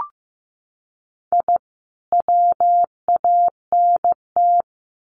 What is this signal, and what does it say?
Morse code, heard as a keyed tone, about 15 wpm, 700 Hz. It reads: I WANT
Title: Sets of 2 Words 15wpm